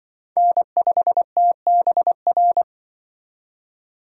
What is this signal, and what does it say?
Morse code, heard as a keyed tone, about 25 wpm, 700 Hz. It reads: N5TBR
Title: Speed Racing - USA 1x3 Call Signs 20wpm